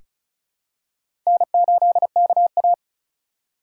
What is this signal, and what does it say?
Morse code, heard as a keyed tone, about 36 wpm, 700 Hz. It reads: N8KA
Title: USA Call Signs - 4 Characters - 4x Character Spacing - 35wpm